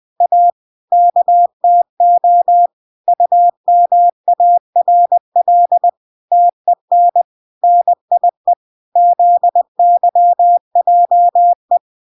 Morse code, heard as a keyed tone, about 20 wpm, 700 Hz. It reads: A KTO UMARL TEN NIE ZYJE